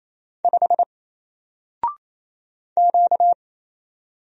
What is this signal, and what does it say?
Morse code, heard as a keyed tone, about 27 wpm, 700 Hz. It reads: 5 Q
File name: Single-Letter-Number-28wpm